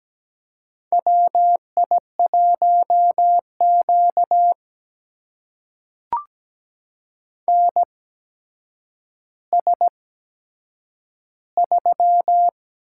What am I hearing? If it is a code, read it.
Morse code, heard as a keyed tone, about 17 wpm, spaced as for 8 wpm, 700 Hz. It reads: WI1Q NS3
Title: USA Call Signs - 4 Characters - 8x Character Spacing - 17wpm